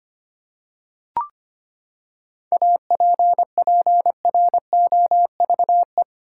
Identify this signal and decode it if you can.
Morse code, heard as a keyed tone, about 25 wpm, 700 Hz. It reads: APPROVE
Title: Seven Letter Words - 25wpm